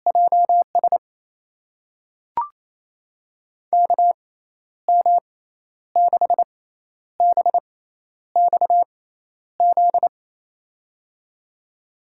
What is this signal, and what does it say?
Morse code, heard as a keyed tone, about 28 wpm, spaced as for 11 wpm, 700 Hz. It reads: JS KM6BXZ
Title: USA Call Signs - 6 Characters - 6x Character Spacing - 28wpm